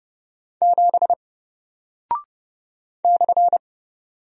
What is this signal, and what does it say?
Morse code, heard as a keyed tone, about 30 wpm, 700 Hz. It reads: 7 /
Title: Single Character (Letter:Number:Punctuation:Prosign) 30wpm